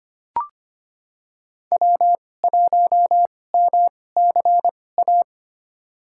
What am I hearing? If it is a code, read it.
Morse code, heard as a keyed tone, about 25 wpm, spaced as for 18 wpm, 700 Hz. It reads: W1MCA
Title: USA Call Signs - 5 Characters - 2x Character Spacing - 25wpm